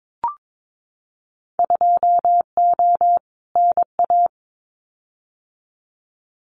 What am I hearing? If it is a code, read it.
Morse code, heard as a keyed tone, about 22 wpm, 700 Hz. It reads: 2O NA
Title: Winter Field Day Exchanges 22wpm